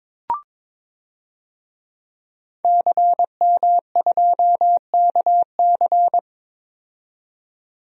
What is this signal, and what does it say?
Morse code, heard as a keyed tone, about 22 wpm, 700 Hz. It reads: CM2KC
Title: DXCC-100 Call Signs - 5 Characters - 22wpm